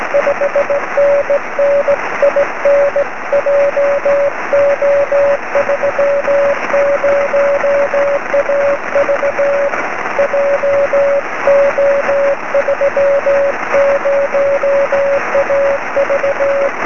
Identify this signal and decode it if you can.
Morse code, heard as a keyed tone, about 18 wpm, 570 Hz, with noise. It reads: 5NN IN JO30AV JO30AV
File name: sm7lcb / 20 23cm / 2059 pa7dw 23cm